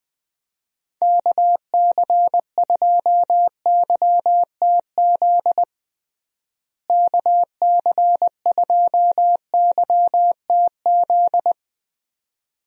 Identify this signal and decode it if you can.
Morse code, heard as a keyed tone, about 21 wpm, 700 Hz. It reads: KC2YTZ KC2YTZ
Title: USA Call Signs - 6 Characters - Repeated 2x - 20wpm